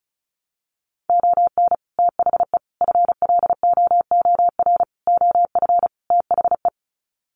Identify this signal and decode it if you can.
Morse code, heard as a keyed tone, about 34 wpm, 700 Hz. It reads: ON THE FLOOR OF THE
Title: Sets of 5 Words 35wpm